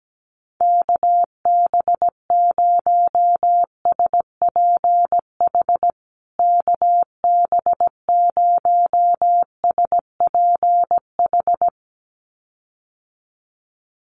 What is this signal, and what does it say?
Morse code, heard as a keyed tone, about 17 wpm, 700 Hz. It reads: KB0SPH KB0SPH
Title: USA Call Signs Repeated 2x 17wpm